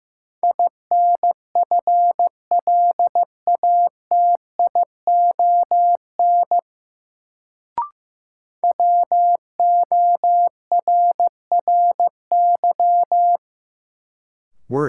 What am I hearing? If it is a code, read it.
Morse code, heard as a keyed tone, about 15 wpm, 700 Hz. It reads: INFLATION WORRY
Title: Sentences from Top 1100 Words - New Word Drill - 15wpm